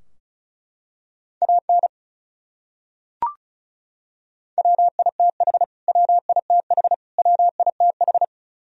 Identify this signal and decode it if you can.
Morse code, heard as a keyed tone, about 36 wpm, 700 Hz. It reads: AN WITH WITH WITH
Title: Top 100 Words Repeated 35wpm